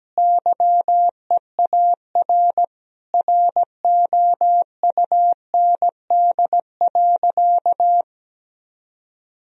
Morse code, heard as a keyed tone, about 17 wpm, 700 Hz. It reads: YEAR ROUND.
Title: Sentences from Top 1000 Words - Review Part 1 - 17wpm